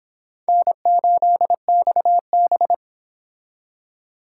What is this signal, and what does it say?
Morse code, heard as a keyed tone, about 26 wpm, 700 Hz. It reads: N8XB